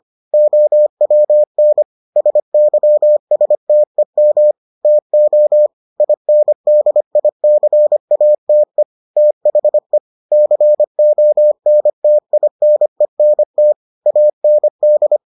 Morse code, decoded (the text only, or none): OWN SYSTEM TO INDICATE THE CONTINENT AND